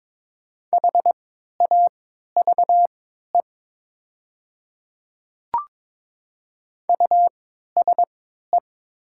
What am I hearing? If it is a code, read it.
Morse code, heard as a keyed tone, about 22 wpm, spaced as for 13 wpm, 700 Hz. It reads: HAVE USE
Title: 3x Character Spacing - Top 100 Words - 22wpm